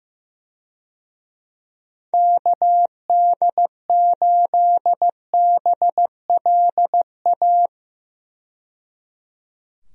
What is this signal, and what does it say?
Morse code, heard as a keyed tone, about 15 wpm, 700 Hz. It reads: KD8BLA
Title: USA Call Signs - 6 Characters - 15wpm